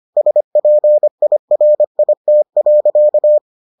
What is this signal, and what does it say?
Morse code, heard as a keyed tone, about 25 wpm, 600 Hz. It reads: SPIRIT.